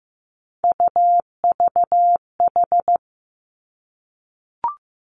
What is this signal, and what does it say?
Morse code, heard as a keyed tone, about 15 wpm, 700 Hz. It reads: UVH
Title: Three Challenging Letter-Number Combinations 15wpm